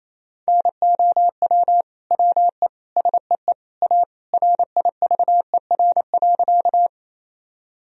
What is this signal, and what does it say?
Morse code, heard as a keyed tone, about 28 wpm, 700 Hz. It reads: NOW WE SEE A RIVER.
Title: Sentences from Top 1000 Words - Review Part 1 - 28wpm